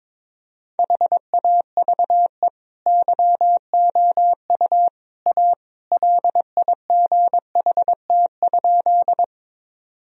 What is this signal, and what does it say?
Morse code, heard as a keyed tone, about 22 wpm, 700 Hz. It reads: HAVE YOU A LIGHT?
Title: Sentences from Top 1000 Words - Review Part 1 - 22wpm